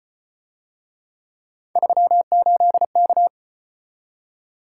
Morse code, heard as a keyed tone, about 33 wpm, 700 Hz. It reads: 38K